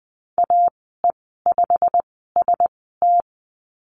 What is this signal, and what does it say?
Morse code, heard as a keyed tone, about 20 wpm, spaced as for 15 wpm, 700 Hz. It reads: AE5ST
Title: 088 Lesson 15g - Number 5 - Call Signs 20wpm